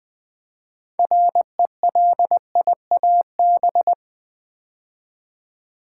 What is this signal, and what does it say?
Morse code, heard as a keyed tone, about 20 wpm, 700 Hz. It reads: RELIAB
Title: Build A Word - 3 Letters At A Time - 8-Letter Words - 20wpm